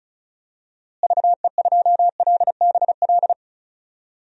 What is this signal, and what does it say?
Morse code, heard as a keyed tone, about 34 wpm, 700 Hz. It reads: VE2LBL
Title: DXCC-100 Call Signs - 6 Characters - Repeated 3x - 35wpm